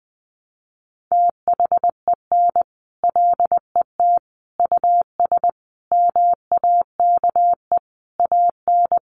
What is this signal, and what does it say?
Morse code, heard as a keyed tone, about 20 wpm, 700 Hz. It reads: THEN LET US MAKE AN
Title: Sentences from Top 1200 Words 20wpm